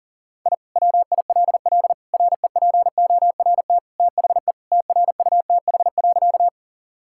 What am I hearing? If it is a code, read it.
Morse code, heard as a keyed tone, about 37 wpm, 700 Hz. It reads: I WILL REPORT THE TRUTH.
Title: Sentences from Top 1000 Words - Review Part 6 - 40wpm